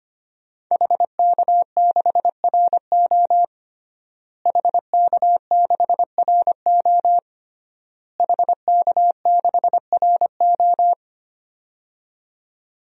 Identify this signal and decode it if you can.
Morse code, heard as a keyed tone, about 25 wpm, 700 Hz. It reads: HK6RO HK6RO HK6RO
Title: DXCC-100 Call Signs - 5 Characters - Repeated 3x - 25wpm